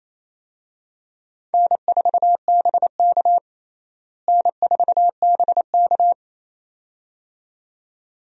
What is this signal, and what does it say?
Morse code, heard as a keyed tone, about 28 wpm, 700 Hz. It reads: N4BK N4BK
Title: USA Call Signs - 4 Characters - Repeated 2x - 28wpm